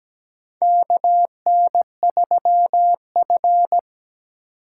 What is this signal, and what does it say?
Morse code, heard as a keyed tone, about 17 wpm, 700 Hz. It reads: KN3F
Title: USA Call Signs - 4 Characters - 2x Character Spacing - 17wpm